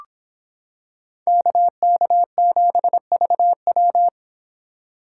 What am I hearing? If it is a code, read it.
Morse code, heard as a keyed tone, about 26 wpm, 700 Hz. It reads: KK7VW